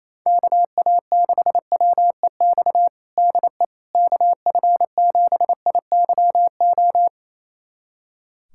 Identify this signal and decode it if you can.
Morse code, heard as a keyed tone, about 28 wpm, 700 Hz. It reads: KA6WEX DE KF7IYO
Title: USA Callsign DE Callsign - 6 Characters - 28wpm